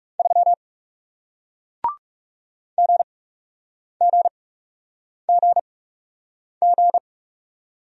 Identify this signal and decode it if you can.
Morse code, heard as a keyed tone, about 40 wpm, spaced as for 8 wpm, 700 Hz. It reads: 3 GGGG